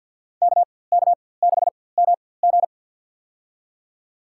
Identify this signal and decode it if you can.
Morse code, heard as a keyed tone, about 47 wpm, spaced as for 23 wpm, 700 Hz. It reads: KK6MG
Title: USA Call Signs - 5 Characters - 4x Character Spacing - 50wpm